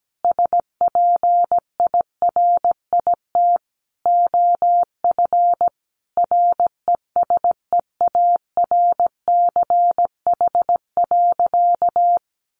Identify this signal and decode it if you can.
Morse code, heard as a keyed tone, about 17 wpm, 700 Hz. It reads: SPIRIT OF RESEARCH.
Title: Sentences from Top 1400 Words 17wpm